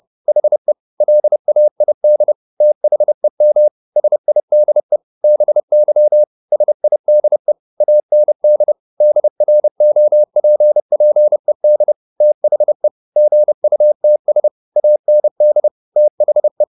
Morse code, heard as a keyed tone, about 30 wpm, 600 Hz. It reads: HE LAID THEM SIDE BY SIDE AND DROPPED THE GUTS AND THE